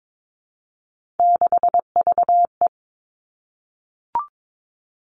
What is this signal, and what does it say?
Morse code, heard as a keyed tone, about 22 wpm, 700 Hz. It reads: 6VE
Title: Three Challenging Letter-Number Combinations 22wpm